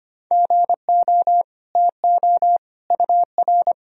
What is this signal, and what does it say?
Morse code, heard as a keyed tone, about 25 wpm, 700 Hz. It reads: GO TO UR
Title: Sentences from Top 300 Words and Abbreviations 25wpm